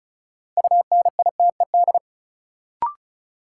Morse code, heard as a keyed tone, about 36 wpm, 700 Hz. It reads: UNITED
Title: Sets of 2 Words - New Word Drill - 35wpm